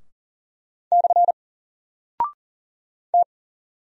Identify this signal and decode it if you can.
Morse code, heard as a keyed tone, about 40 wpm, 700 Hz. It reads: / T
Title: Single Character (Letter:Number:Punctuation:Prosign) Alt 40wpm